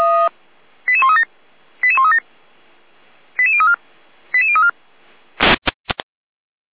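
Five-tone selective call (ZVEI), heard as quick runs of five tones, tones of about 70 ms. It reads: 88137 88137 88034 88034
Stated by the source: ZVEI five-tone selective call 88190, received by SDR.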